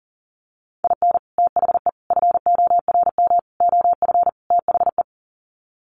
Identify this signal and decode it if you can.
Morse code, heard as a keyed tone, about 40 wpm, 700 Hz. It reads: IN THE FORM OF THE
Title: Sets of 5 Words Encore2 40wpm